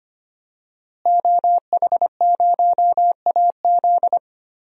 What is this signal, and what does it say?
Morse code, heard as a keyed tone, about 25 wpm, 700 Hz. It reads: OH0AZ
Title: DXCC-100 Call Signs - 5 Characters - Repeated 3x - 25wpm